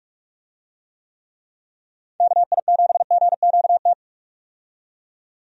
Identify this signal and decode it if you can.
Morse code, heard as a keyed tone, about 43 wpm, 700 Hz. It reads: KI7GQT
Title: USA Call Signs - 6 Characters - 45wpm